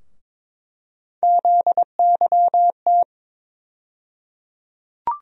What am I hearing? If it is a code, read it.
Morse code, heard as a keyed tone, about 22 wpm, 700 Hz. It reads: ZYT
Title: USA Call Sign Postfixes - 3 Letters 22wpm